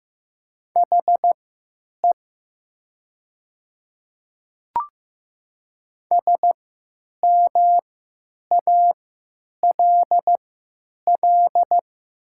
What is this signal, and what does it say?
Morse code, heard as a keyed tone, about 15 wpm, spaced as for 9 wpm, 700 Hz. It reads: HE SMALL
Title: 3x Character Spacing - Top 200 Words - 15wpm